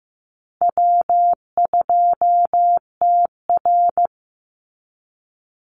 Morse code, heard as a keyed tone, about 15 wpm, 700 Hz. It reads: W2TR
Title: Easy Call Signs Repeated 3x - 15wpm